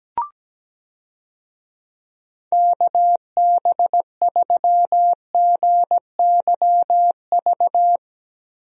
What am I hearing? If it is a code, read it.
Morse code, heard as a keyed tone, about 17 wpm, 700 Hz. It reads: KB3GYV
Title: USA Call Signs - 6 Characters - Repeated 2x - 17wpm